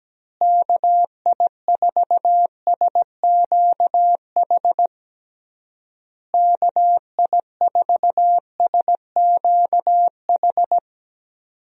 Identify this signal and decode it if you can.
Morse code, heard as a keyed tone, about 17 wpm, 700 Hz. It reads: KI4SQH KI4SQH
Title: USA Call Signs - 6 Characters - Repeated 3x - 17wpm